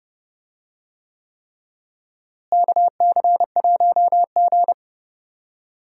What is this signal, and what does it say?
Morse code, heard as a keyed tone, about 30 wpm, 700 Hz. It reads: KC1G
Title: USA Call Signs - 4 Characters - Repeated 3x - 30wpm